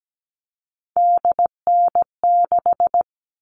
Morse code, heard as a keyed tone, about 17 wpm, 700 Hz. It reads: DN6
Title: All Cut Numbers - 3-Digit 17wpm